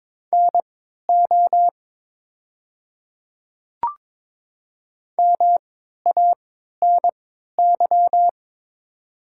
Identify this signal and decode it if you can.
Morse code, heard as a keyed tone, about 22 wpm, spaced as for 12 wpm, 700 Hz. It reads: NO MANY
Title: 3x Character Spacing - Top 100 Words - 22wpm